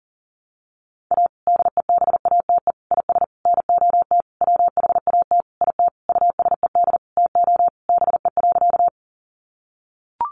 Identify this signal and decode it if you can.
Morse code, heard as a keyed tone, about 37 wpm, 700 Hz. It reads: A DEBATE IS NOT WHAT IT USED TO BE.
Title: Sentences from Top 1100 Words 40wpm